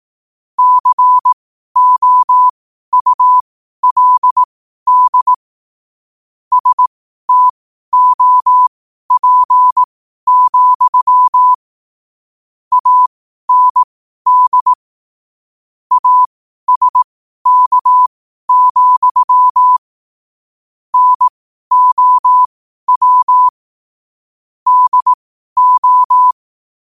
Morse code, heard as a keyed tone, about 18 wpm, spaced as for 12 wpm, 1000 Hz. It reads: COULD STOP, AND ASK, NOW DO
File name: qotdfiles / 12wpm